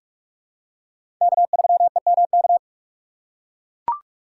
Morse code, heard as a keyed tone, about 41 wpm, 700 Hz. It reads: K3EMK
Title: USA Call Signs - 5 Characters - 3x Character Spacing - 45wpm